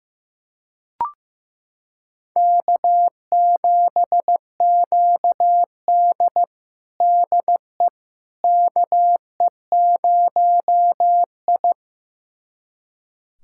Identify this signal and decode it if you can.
Morse code, heard as a keyed tone, about 15 wpm, 700 Hz. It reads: K7QD DE KE0I
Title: USA Callsign DE Callsign - 4 Characters - 15wpm